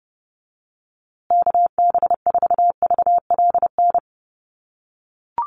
Morse code, heard as a keyed tone, about 30 wpm, 700 Hz. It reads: KB4VLN
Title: USA Call Signs Repeated 2x 30wpm